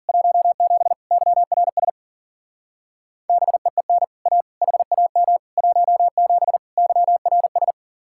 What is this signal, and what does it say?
Morse code, heard as a keyed tone, about 49 wpm, 700 Hz. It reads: 17 YRS BEEN A HAM 17 YRS
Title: Speed-Racing Common QSO Phrases Encore - 40wpm